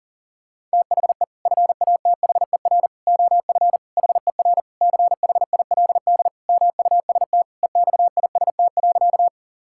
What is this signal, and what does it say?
Morse code, heard as a keyed tone, about 40 wpm, 700 Hz. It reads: THE FATHER OF HER CHILD MUST EXIST.
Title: Sentences from Top 1000 Words - Review Part 5 - 40wpm